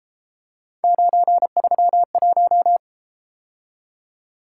Morse code, heard as a keyed tone, about 33 wpm, 700 Hz. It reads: 931